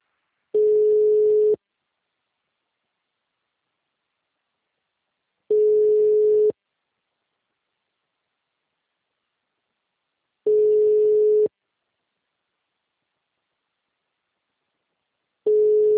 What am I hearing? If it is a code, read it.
Ringback tone (Europe, 425 Hz).
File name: out- -2012- - - .23034